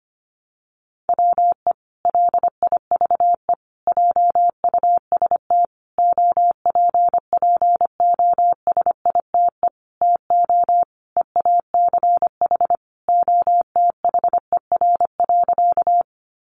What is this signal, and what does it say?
Morse code, heard as a keyed tone, about 25 wpm, 700 Hz. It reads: WE LIVE JUST OPPOSITE TO EACH OTHER.
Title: Sentences from Top 1300 Words 25wpm